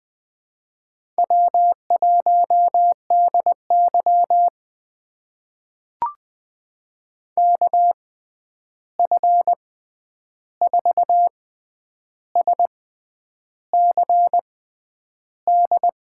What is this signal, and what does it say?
Morse code, heard as a keyed tone, about 20 wpm, spaced as for 9 wpm, 700 Hz. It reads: W1DY KF4SCD
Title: USA Call Signs - 6x Character Spacing - 20wpm